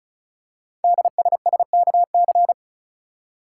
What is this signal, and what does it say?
Morse code, heard as a keyed tone, about 36 wpm, 700 Hz. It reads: DSSKC
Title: Code Groups - 5 Letters - 35wpm